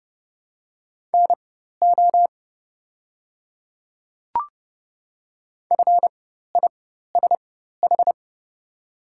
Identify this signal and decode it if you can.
Morse code, heard as a keyed tone, about 30 wpm, spaced as for 14 wpm, 700 Hz. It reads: NO FISH
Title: 4x Character Spacing - Top 300 Words - 30wpm